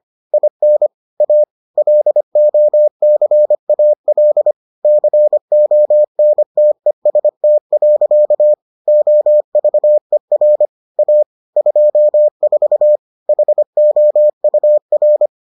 Morse code, heard as a keyed tone, about 25 wpm, 600 Hz. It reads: IN A LOCAL CONTEST. OVER A 24 HOUR